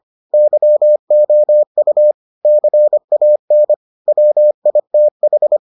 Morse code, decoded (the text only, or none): YOU CAN WITH